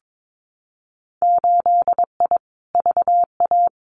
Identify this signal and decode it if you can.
Morse code, heard as a keyed tone, about 22 wpm, 700 Hz. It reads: 8I VA
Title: Winter Field Day Exchanges Encore 22wpm